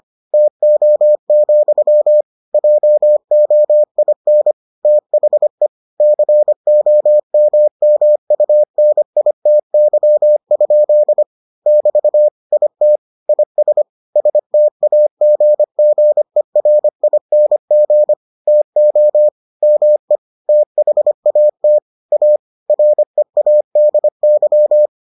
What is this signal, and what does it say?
Morse code, heard as a keyed tone, about 25 wpm, 600 Hz. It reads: TO, JOIN THE COMMUNITY? <BT> IT IS STAGGERING TO ME THAT A READY